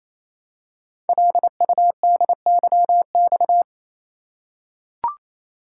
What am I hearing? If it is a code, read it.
Morse code, heard as a keyed tone, about 28 wpm, 700 Hz. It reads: LUDYX
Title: Code Groups - 5 Letters - 28wpm